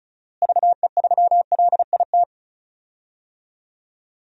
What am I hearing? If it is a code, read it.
Morse code, heard as a keyed tone, about 34 wpm, 700 Hz. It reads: VE3LIT
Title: DXCC-100 Call Signs - 6 Characters - Repeated 3x - 35wpm